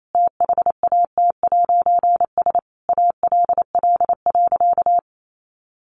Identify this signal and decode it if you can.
Morse code, heard as a keyed tone, about 28 wpm, 700 Hz. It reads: THAT'S ALL.
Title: A Sound Of Thunder 28wpm